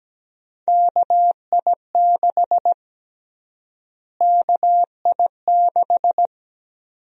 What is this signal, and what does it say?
Morse code, heard as a keyed tone, about 17 wpm, 700 Hz. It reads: KI6 KI6
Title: USA Call Sign Prefixes - Repeated 2x - 17wpm